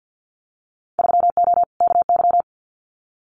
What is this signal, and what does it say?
Morse code, heard as a keyed tone, about 50 wpm, 700 Hz. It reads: 3O KY
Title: Winter Field Day Exchanges 50wpm